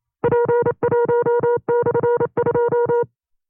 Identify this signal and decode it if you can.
Morse code, heard as a keyed tone, about 29 wpm, 480 Hz. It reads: P1/2
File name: FreeFiles / CW-practice / 25WPM-side2